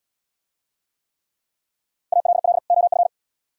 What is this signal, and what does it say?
Morse code, heard as a keyed tone, about 78 wpm, 700 Hz. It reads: ALL OF